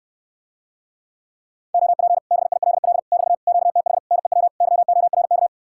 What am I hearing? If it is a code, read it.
Morse code, heard as a keyed tone, about 68 wpm, 700 Hz. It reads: OP BILL <BT> QTH NEW YORK